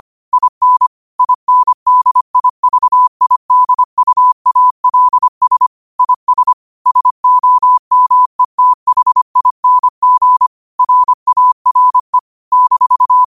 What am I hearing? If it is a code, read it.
Morse code, heard as a keyed tone, about 25 wpm, 1000 Hz. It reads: IN INDIVIDUALS IS SOMETHING RARE -
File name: qotdfiles / 25wpm